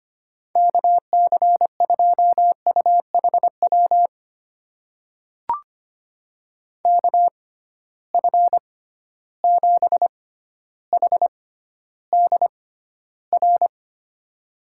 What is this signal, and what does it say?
Morse code, heard as a keyed tone, about 25 wpm, spaced as for 13 wpm, 700 Hz. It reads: KC2UHW KF7HDR
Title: USA Call Signs - 6 Characters - 6x Character Spacing - 25wpm